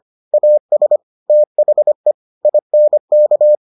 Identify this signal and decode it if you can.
Morse code, heard as a keyed tone, about 25 wpm, 600 Hz. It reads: AS THE INK